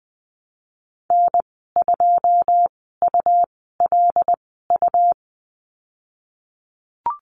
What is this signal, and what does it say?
Morse code, heard as a keyed tone, about 20 wpm, spaced as for 15 wpm, 700 Hz. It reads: N2ULU
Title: 161 Lesson 26g - Letter g - Call Signs 20wpm